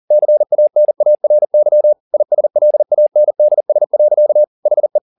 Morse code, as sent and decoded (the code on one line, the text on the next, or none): -.-. .- -. .- .-. -.-- / .. ... .-.. .- -. -.. ... .-.-.- / .... .
CANARY ISLANDS. HE